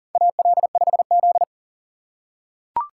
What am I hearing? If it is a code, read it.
Morse code, heard as a keyed tone, about 40 wpm, 700 Hz. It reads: AL5Z
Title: USA Call Signs - 4x Character Spacing - 40wpm